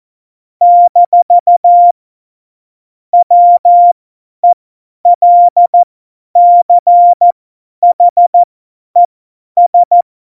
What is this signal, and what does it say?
Morse code, heard as a keyed tone, about 14 wpm, spaced as for 10 wpm, 700 Hz. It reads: - WELCHES
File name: Morsekurs / Witze / Witz 19. September 17, Tempo 50